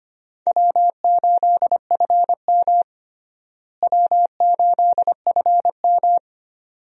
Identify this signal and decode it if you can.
Morse code, heard as a keyed tone, about 25 wpm, 700 Hz. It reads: W8FM W8FM
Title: USA Call Signs - 4 Characters - Repeated 2x - 25wpm